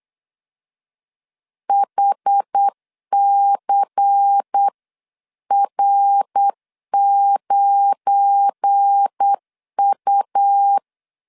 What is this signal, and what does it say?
Morse code, heard as a keyed tone, about 8 wpm, 800 Hz. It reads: HCR9U
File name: LEZIONI / 7 GF9Z / 7 GF9 8wpm 3545